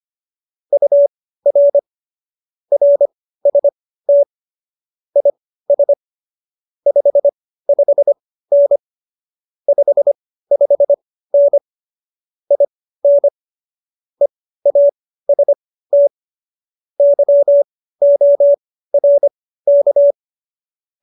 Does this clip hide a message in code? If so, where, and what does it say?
Yes: Morse code from 0.7 to 20.1 s, heard as a keyed tone, about 25 wpm, spaced as for 15 wpm, 570 Hz. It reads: UR RST IS 55N 55N IN EAST YORK